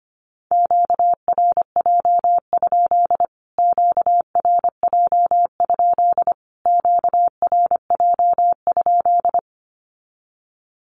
Morse code, heard as a keyed tone, about 25 wpm, 700 Hz. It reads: QRJ? QRJ? QRJ?
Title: Q-Codes Repeated 3x 25wpm